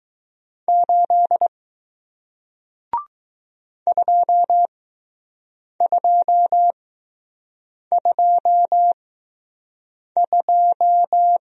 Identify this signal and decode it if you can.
Morse code, heard as a keyed tone, about 20 wpm, spaced as for 6 wpm, 700 Hz. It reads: 8 2222